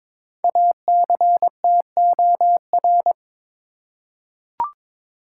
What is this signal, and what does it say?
Morse code, heard as a keyed tone, about 22 wpm, 700 Hz. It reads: ACTOR
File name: 5 Letter Words - First 3 Letters Given - 22wpm